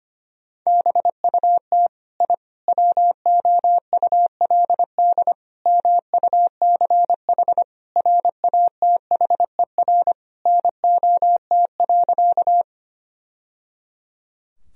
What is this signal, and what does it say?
Morse code, heard as a keyed tone, about 25 wpm, 700 Hz. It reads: BUT I WOULD MUCH RATHER NOT.
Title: Sentences from Top 1000 Words - Review Part 4 - 25wpm